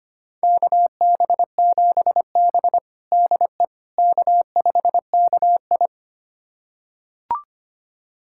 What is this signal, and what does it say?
Morse code, heard as a keyed tone, about 25 wpm, 700 Hz. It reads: KB7B DE K5KI
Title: USA Callsign DE Callsign - 4 Characters - 25wpm